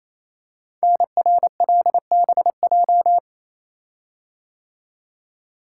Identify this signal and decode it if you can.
Morse code, heard as a keyed tone, about 27 wpm, 700 Hz. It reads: NRLBJ
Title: Code Groups - 5 Letters - 28wpm